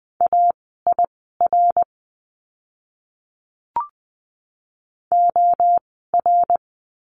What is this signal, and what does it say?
Morse code, heard as a keyed tone, about 20 wpm, spaced as for 15 wpm, 700 Hz. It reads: AIR OR
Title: 068 Lesson 12f - Letter d - Words 20wpm